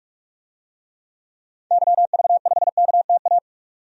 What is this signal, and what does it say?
Morse code, heard as a keyed tone, about 46 wpm, 700 Hz. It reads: YV5KTA
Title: DXCC-100 Call Signs - 6 Characters - 45wpm